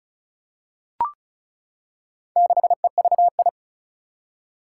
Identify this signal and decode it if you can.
Morse code, heard as a keyed tone, about 36 wpm, 700 Hz. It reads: 6EVI
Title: Four Challenging Letter-Number Combinations 35wpm